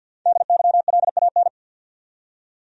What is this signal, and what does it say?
Morse code, heard as a keyed tone, about 53 wpm, 700 Hz. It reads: DYLAN
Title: USA Common Names - Max 5 Letters - 50wpm